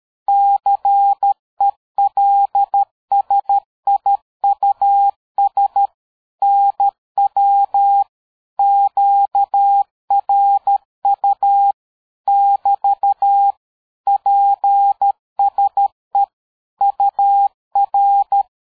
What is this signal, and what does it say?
Morse code, heard as a keyed tone, about 13 wpm, 800 Hz. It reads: CELSIUS NW QRU <BT> PSE UR
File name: dk5ke / QSO 60BpM